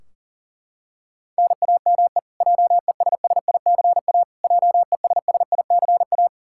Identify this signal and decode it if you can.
Morse code, heard as a keyed tone, about 40 wpm, 700 Hz. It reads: NAME JESSICA JESSICA
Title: Practice QSO - Exchange 1 - Beginning - 40wpm